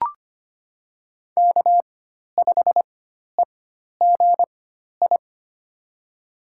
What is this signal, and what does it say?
Morse code, heard as a keyed tone, about 25 wpm, spaced as for 12 wpm, 700 Hz. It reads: K5EGI
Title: USA Call Signs - 5 Characters - 4x Character Spacing - 25wpm